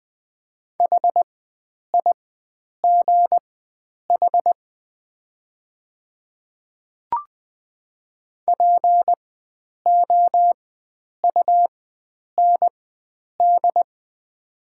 Morse code, heard as a keyed tone, about 20 wpm, spaced as for 9 wpm, 700 Hz. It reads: HIGH POUND
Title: 4x Character Spacing - Top 500 Words - 20wpm